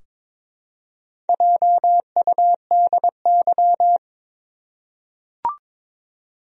Morse code, heard as a keyed tone, about 22 wpm, 700 Hz. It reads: JUDY
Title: USA Common Names - Max 4 Letters - 22wpm